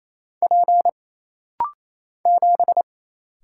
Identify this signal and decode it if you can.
Morse code, heard as a keyed tone, about 28 wpm, 700 Hz. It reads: P 7
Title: Single Letter-Number - Rapid-Fire 28wpm